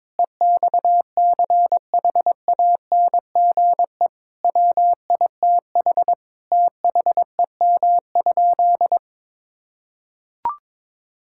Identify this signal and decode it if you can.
Morse code, heard as a keyed tone, about 22 wpm, 700 Hz. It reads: EXCHANGE WITH THEM?
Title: Sentences from Top 1000 Words - Review Part 6 - 22wpm